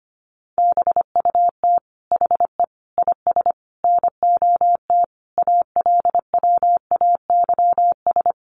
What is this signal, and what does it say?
Morse code, heard as a keyed tone, about 25 wpm, 700 Hz. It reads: BUT HE IS NOT ALWAYS
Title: Sentences from Top 1000 Words 25wpm